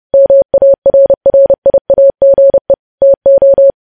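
Morse code, heard as a keyed tone, about 30 wpm, 550 Hz. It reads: MARRIAGE TO
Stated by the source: Morse news headlines sent at 30 wpm.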